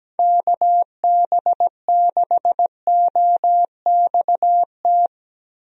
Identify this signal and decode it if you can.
Morse code, heard as a keyed tone, about 17 wpm, 700 Hz. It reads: KB6OXT
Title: USA Call Signs - 6 Characters - Repeated 2x - 17wpm